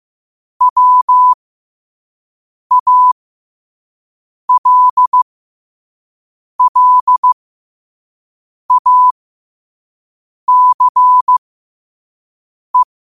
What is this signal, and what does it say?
Morse code, heard as a keyed tone, about 15 wpm, spaced as for 5 wpm, 1000 Hz. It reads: WALLACE
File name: qotdfiles / 5wpm